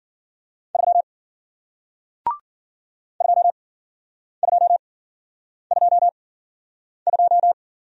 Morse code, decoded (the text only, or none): 3 2222